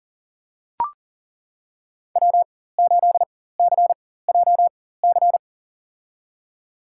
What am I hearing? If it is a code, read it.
Morse code, heard as a keyed tone, about 40 wpm, spaced as for 19 wpm, 700 Hz. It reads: W8CJC
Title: USA Call Signs - 5 Characters - 4x Character Spacing - 40wpm